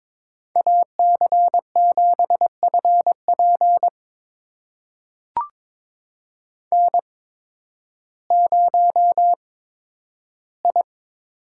Morse code, heard as a keyed tone, about 22 wpm, spaced as for 12 wpm, 700 Hz. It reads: AC7FP N0I
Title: USA Call Signs - 5 Characters - 8x Character Spacing - 22wpm